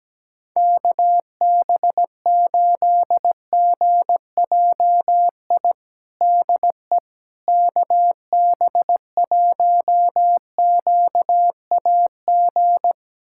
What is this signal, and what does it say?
Morse code, heard as a keyed tone, about 17 wpm, 700 Hz. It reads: KB8GJI DE KB1QAG
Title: USA Callsign DE Callsign 17wpm